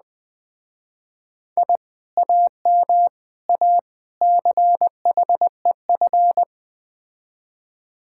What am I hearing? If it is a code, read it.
Morse code, heard as a keyed tone, about 20 wpm, 700 Hz. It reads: I AM A CHEF
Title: Speed-Racing Common QSO Phrases - 17wpm